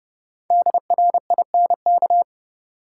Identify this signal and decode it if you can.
Morse code, heard as a keyed tone, about 30 wpm, 700 Hz. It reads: DRINK
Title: Five Letter Words - 30wpm